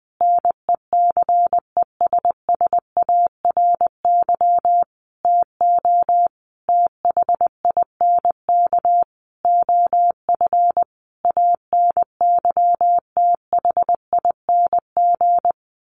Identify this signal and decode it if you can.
Morse code, heard as a keyed tone, about 20 wpm, 700 Hz. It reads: NECESSARY TO THINK OF ANYTHING
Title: Sentences from Top 1300 Words 20wpm